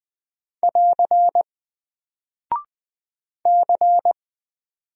Morse code, heard as a keyed tone, about 20 wpm, 700 Hz. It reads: <AR> C
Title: Single Character (Letter:Number:Punctuation:Prosign) 20wpm